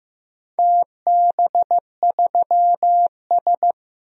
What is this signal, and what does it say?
Morse code, heard as a keyed tone, about 15 wpm, 700 Hz. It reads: TB3S
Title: DXCC-100 Call Signs - 4 Characters - 15wpm